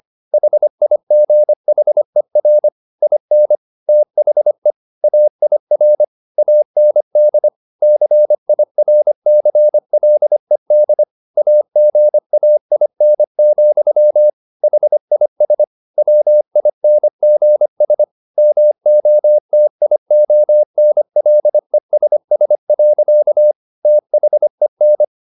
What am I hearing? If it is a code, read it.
Morse code, heard as a keyed tone, about 25 wpm, 600 Hz. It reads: HIGHER IN THE AIR AND CIRCLED AGAIN, HIS WINGS MOTIONLESS. THEN
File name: the old man and the sea part 1 25wpm